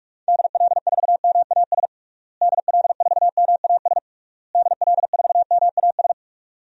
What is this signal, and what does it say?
Morse code, heard as a keyed tone, about 46 wpm, 700 Hz. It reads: DL4MAS DL4MAS DL4MAS
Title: DXCC-100 Call Signs - 6 Characters - Repeated 3x - 45wpm